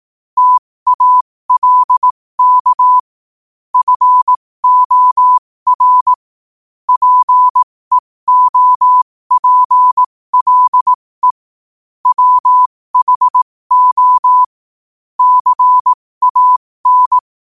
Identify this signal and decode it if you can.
Morse code, heard as a keyed tone, about 18 wpm, spaced as for 15 wpm, 1000 Hz. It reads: TALK FOR PEOPLE WHO CAN
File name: qotdfiles / 15wpm